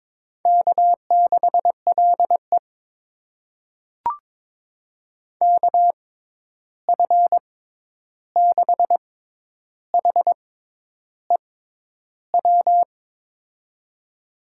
Morse code, heard as a keyed tone, about 21 wpm, spaced as for 10 wpm, 700 Hz. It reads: K6LE KF6HEW
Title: USA Call Signs - 6x Character Spacing - 22wpm